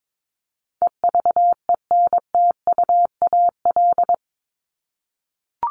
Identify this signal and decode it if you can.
Morse code, heard as a keyed tone, about 22 wpm, 700 Hz. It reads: EVENTUAL
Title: LY Suffix - Top 100 Words with Base Words 22wpm